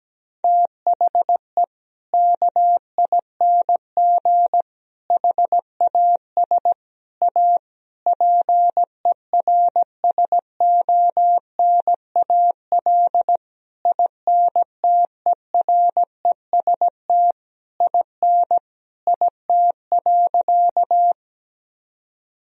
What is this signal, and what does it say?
Morse code, heard as a keyed tone, about 17 wpm, 700 Hz. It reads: THE KING HAS A PERSONAL INTEREST IN IT.
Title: Sentences from Top 1000 Words - Review Part 4 - 17wpm